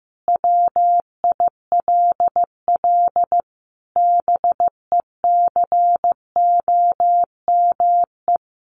Morse code, heard as a keyed tone, about 15 wpm, 700 Hz. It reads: WILL BECOME
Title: Sentences from Top 1100 Words 15wpm